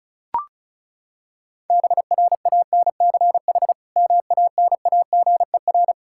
Morse code, heard as a keyed tone, about 36 wpm, 700 Hz. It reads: BRANCH MANAGER
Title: K1USN SST QSO Practice 35wpm